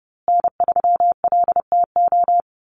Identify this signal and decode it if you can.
Morse code, heard as a keyed tone, about 30 wpm, 700 Hz. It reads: N3LTO
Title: USA Call Signs Repeated 3x 30wpm